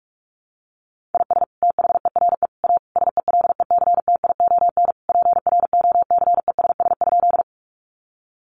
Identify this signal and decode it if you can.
Morse code, heard as a keyed tone, about 44 wpm, 700 Hz. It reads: IS THERE A SELECTION PROCESS?
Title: Sentences from Top 1400 Words 45wpm